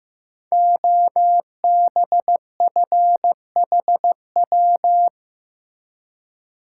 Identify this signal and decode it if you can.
Morse code, heard as a keyed tone, about 15 wpm, 700 Hz. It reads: OBFHW
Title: Code Groups - 5 Letters - 15wpm